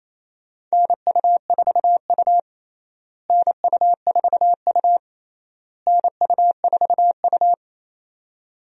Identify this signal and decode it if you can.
Morse code, heard as a keyed tone, about 28 wpm, 700 Hz. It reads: NU4U NU4U NU4U
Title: USA Call Signs - 4 Characters - Repeated 3x - 28wpm